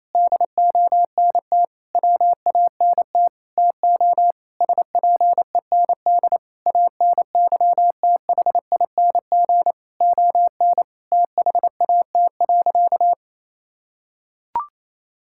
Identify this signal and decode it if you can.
Morse code, heard as a keyed tone, about 28 wpm, 700 Hz. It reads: DONT WANT TO SPEND ANYTHING ON THAT.
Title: Sentences from Top 1000 Words - Review Part 5 - 28wpm